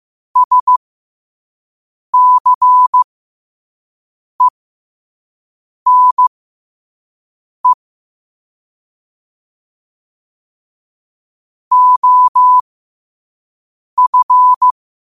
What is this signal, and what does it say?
Morse code, heard as a keyed tone, about 15 wpm, spaced as for 5 wpm, 1000 Hz. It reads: SCENE OF
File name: qotdfiles / 5wpm